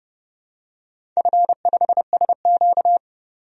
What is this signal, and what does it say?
Morse code, heard as a keyed tone, about 31 wpm, 700 Hz. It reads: F5SQ
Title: DXCC-100 Call Signs - 4 Characters - 30wpm